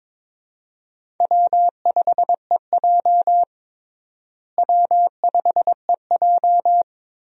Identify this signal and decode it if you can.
Morse code, heard as a keyed tone, about 22 wpm, 700 Hz. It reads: W5EJ W5EJ
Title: USA Call Signs - 4 Characters - Repeated 3x - 22wpm